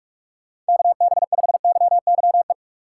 Morse code, heard as a keyed tone, about 41 wpm, 700 Hz. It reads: KB5YYE